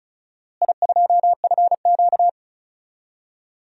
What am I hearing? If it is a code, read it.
Morse code, heard as a keyed tone, about 36 wpm, 700 Hz. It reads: I2FQ
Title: DXCC-100 Call Signs - 4 Characters - Repeated 3x - 35wpm